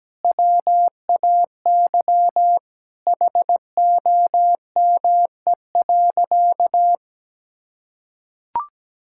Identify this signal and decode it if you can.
Morse code, heard as a keyed tone, about 17 wpm, 700 Hz. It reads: WAY HOME.
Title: Sentences from Top 1600 Words 17wpm